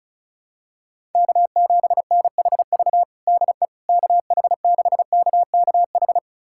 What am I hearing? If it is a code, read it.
Morse code, heard as a keyed tone, about 36 wpm, 700 Hz. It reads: K7NHV DE KH6KKH
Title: USA Callsign DE Callsign 35wpm